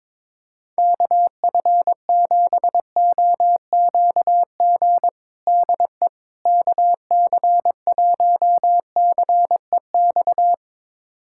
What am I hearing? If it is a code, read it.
Morse code, heard as a keyed tone, about 22 wpm, 700 Hz. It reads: KF7OQG DE KC1CEX
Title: USA Callsign DE Callsign - 6 Characters - 22wpm